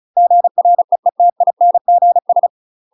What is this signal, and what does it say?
Morse code, heard as a keyed tone, about 34 wpm, 700 Hz. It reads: GREETINGS
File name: UK MASTER QSOs TRAINING / 35WPM / UK MASTER QSOs 0025